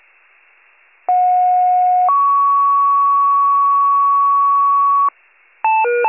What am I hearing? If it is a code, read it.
Two-tone page, heard as a one-second tone, then a three-second tone, about 730 then 1120 Hz. It is heard from 1.1 to 5.1 s.